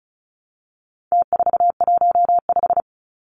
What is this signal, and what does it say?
Morse code, heard as a keyed tone, about 34 wpm, 700 Hz. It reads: T415
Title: Numbers - Contest Style - 4-Digit 35wpm